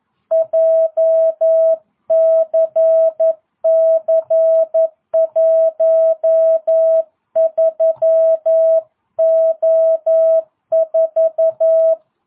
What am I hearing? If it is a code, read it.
Morse code, heard as a keyed tone, about 11 wpm, 650 Hz. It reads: JCC13O4